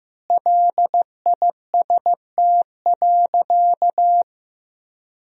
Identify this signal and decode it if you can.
Morse code, heard as a keyed tone, about 15 wpm, 700 Hz. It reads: LIST.
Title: Sentences from Top 1000 Words - Review Part 2 - 15wpm